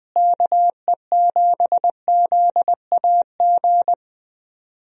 Morse code, heard as a keyed tone, about 20 wpm, 700 Hz. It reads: KE7ZAG
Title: USA Call Signs - 6 Characters - Repeated 3x - 20wpm